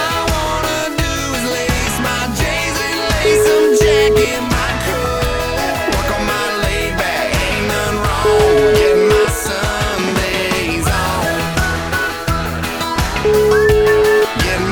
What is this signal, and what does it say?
Ringback tone (Europe, 425 Hz), with noise.